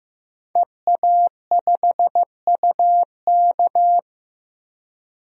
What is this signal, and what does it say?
Morse code, heard as a keyed tone, about 16 wpm, 700 Hz. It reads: EA5UK
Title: DXCC-100 Call Signs - 5 Characters - Repeated 3x - 15wpm